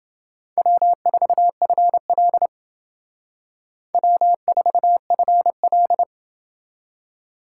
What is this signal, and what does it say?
Morse code, heard as a keyed tone, about 28 wpm, 700 Hz. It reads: W4FL W4FL